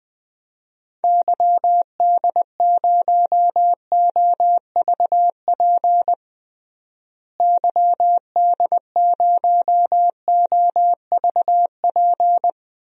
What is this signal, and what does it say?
Morse code, heard as a keyed tone, about 20 wpm, 700 Hz. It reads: YD0OVP YD0OVP
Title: DXCC-100 Call Signs - 6 Characters - Repeated 3x - 20wpm